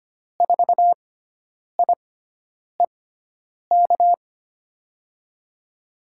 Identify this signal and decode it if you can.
Morse code, heard as a keyed tone, about 25 wpm, spaced as for 9 wpm, 700 Hz. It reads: 4IEK
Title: USA Call Signs - 6 Characters - 6x Character Spacing - 25wpm